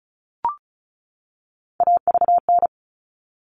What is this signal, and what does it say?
Morse code, heard as a keyed tone, about 34 wpm, 700 Hz. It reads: AVN
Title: All Cut Numbers - 3-Digit 35wpm